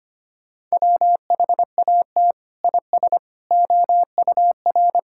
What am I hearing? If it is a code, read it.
Morse code, heard as a keyed tone, about 25 wpm, 700 Hz. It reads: WHAT IS OUR
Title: Sentences from Top 1000 Words - Review Part 6 - 25wpm